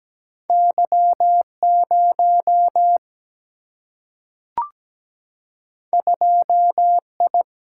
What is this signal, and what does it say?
Morse code, heard as a keyed tone, about 17 wpm, 700 Hz. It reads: Y0 2I
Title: Single-Letter Single-Number 17wpm